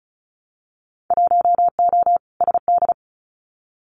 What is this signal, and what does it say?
Morse code, heard as a keyed tone, about 34 wpm, 700 Hz. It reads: 1O SD
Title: Winter Field Day Exchanges Encore 35wpm